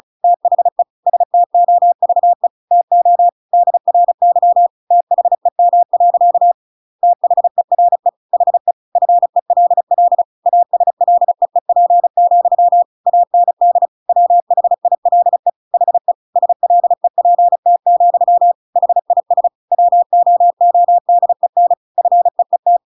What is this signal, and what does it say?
Morse code, heard as a keyed tone, about 34 wpm, 700 Hz. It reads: THE STOVE TO DRY THEM. THERE HE FELL ASLEEP, AND WHILE HE SLEPT, HIS WOODEN FEET